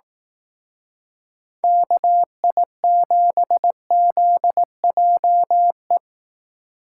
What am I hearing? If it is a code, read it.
Morse code, heard as a keyed tone, about 18 wpm, 700 Hz. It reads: KI7ZJE
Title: Speed Racing - USA 2x3 Call Signs 15wpm